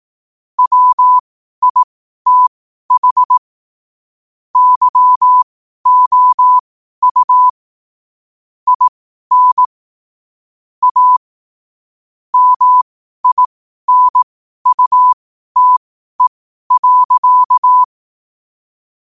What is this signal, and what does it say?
Morse code, heard as a keyed tone, about 18 wpm, spaced as for 12 wpm, 1000 Hz. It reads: WITH YOU IN A MINUTE.
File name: qotdfiles / 12wpm